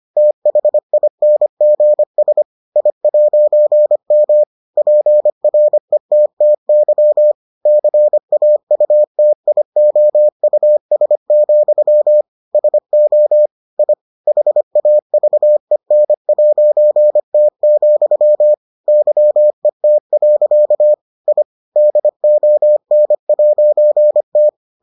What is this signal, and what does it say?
Morse code, heard as a keyed tone, about 25 wpm, 600 Hz. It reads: THINGS I'M PRETTY CAUTIOUS, SO I HAVEN'T, YET. I DON'T